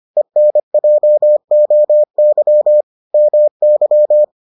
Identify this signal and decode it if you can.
Morse code, heard as a keyed tone, about 25 wpm, 600 Hz. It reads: ENJOY MY